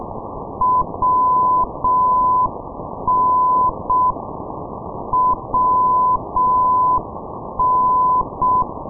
Morse code, heard as a keyed tone, about 6 wpm, 1010 Hz, with noise. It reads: WN WN
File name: ndb minsk - machulishchi wn-458, sdr kiwi minsk